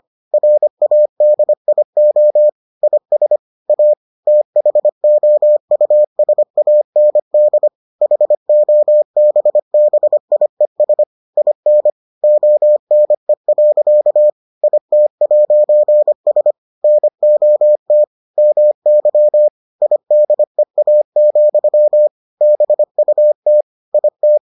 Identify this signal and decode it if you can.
Morse code, heard as a keyed tone, about 25 wpm, 600 Hz. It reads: RADIO IS A THOUSAND HOBBIES IN ONE. IT'S NOT MY IDEA, BUT IT